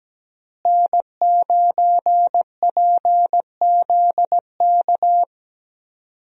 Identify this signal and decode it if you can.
Morse code, heard as a keyed tone, about 17 wpm, 700 Hz. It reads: N9PZK
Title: USA Call Signs - 5 Characters - Repeated 3x - 17wpm